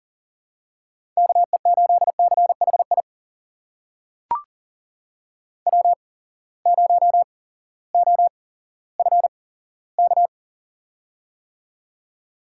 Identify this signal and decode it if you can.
Morse code, heard as a keyed tone, about 40 wpm, spaced as for 18 wpm, 700 Hz. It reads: KE8CHI W0OFK
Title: USA Call Signs - 8x Character Spacing - 40wpm